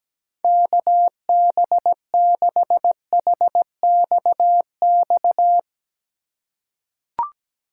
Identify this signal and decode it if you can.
Morse code, heard as a keyed tone, about 17 wpm, 700 Hz. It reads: KB6HXX
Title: USA Call Signs - 6 Characters - 2x Character Spacing - 17wpm